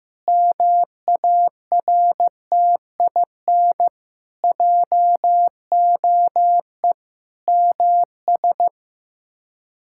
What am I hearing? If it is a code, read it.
Morse code, heard as a keyed tone, about 16 wpm, 700 Hz. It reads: MARTIN JOE MS
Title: K1USN SST QSO Practice - 15wpm